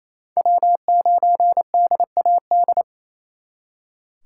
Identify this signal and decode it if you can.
Morse code, heard as a keyed tone, about 36 wpm, 700 Hz. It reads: W9DAD